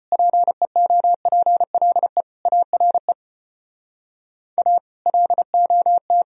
Morse code, heard as a keyed tone, about 33 wpm, 700 Hz. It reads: PEOPLE ARE A LOT